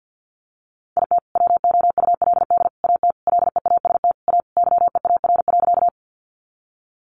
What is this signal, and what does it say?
Morse code, heard as a keyed tone, about 50 wpm, 700 Hz. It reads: IT WOULD AT LEAST A YEAR.
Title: Sentences from Top 700 Words 50wpm